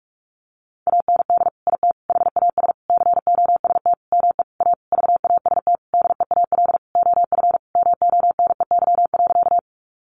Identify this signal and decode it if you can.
Morse code, heard as a keyed tone, about 44 wpm, 700 Hz. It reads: AND IT HAS COST ME A VAST DEAL OF MONEY.
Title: Sentences from Top 1300 Words 45wpm